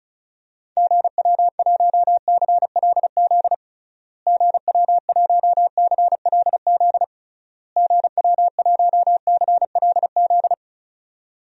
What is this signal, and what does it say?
Morse code, heard as a keyed tone, about 36 wpm, 700 Hz. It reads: GW1CLZ GW1CLZ GW1CLZ
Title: DXCC-100 Call Signs - 6 Characters - Repeated 3x - 35wpm